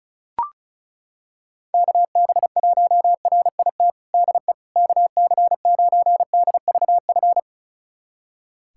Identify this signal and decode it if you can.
Morse code, heard as a keyed tone, about 36 wpm, 700 Hz. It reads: KB1RIT DE KC9DVF
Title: USA Callsign DE Callsign - 6 Characters - 35wpm